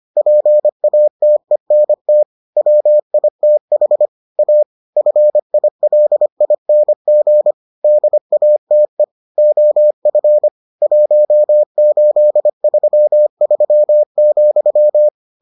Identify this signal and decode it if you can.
Morse code, heard as a keyed tone, about 25 wpm, 600 Hz. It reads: PATENT WITH A FILING DATE OF 1833,